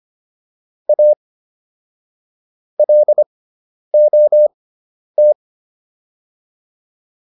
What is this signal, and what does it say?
Morse code, heard as a keyed tone, about 25 wpm, spaced as for 10 wpm, 600 Hz. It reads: A LOT